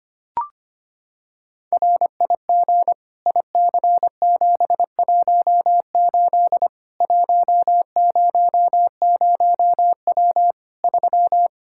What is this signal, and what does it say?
Morse code, heard as a keyed tone, about 25 wpm, 700 Hz. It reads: RIG IC718 100W 3
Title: Practice QSO - Exchange 2 - Info - 25wpm